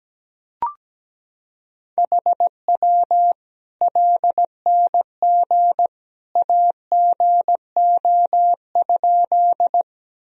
Morse code, heard as a keyed tone, about 17 wpm, 700 Hz. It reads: HW LNG AGO?
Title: Sentences from Top 500 Words and Abbreviations 17wpm